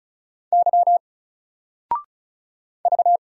Morse code, heard as a keyed tone, about 34 wpm, 700 Hz. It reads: Y V
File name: Single-Letter-Number-35wpm